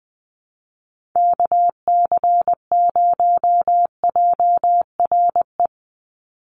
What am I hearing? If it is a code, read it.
Morse code, heard as a keyed tone, about 20 wpm, 700 Hz. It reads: KC0JRE
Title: USA Call Signs 20wpm